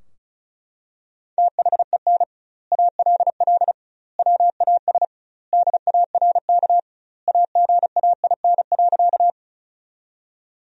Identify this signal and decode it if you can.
Morse code, heard as a keyed tone, about 34 wpm, 700 Hz. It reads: THEN ALL WAS DARK AGAIN.
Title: 2x Word-Spacing - Sentences from Top 500 Words - 35wpm